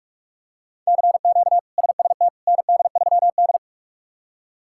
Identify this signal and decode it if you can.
Morse code, heard as a keyed tone, about 46 wpm, 700 Hz. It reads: CQ SST ND3D
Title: K1USN SST QSO Practice - 45wpm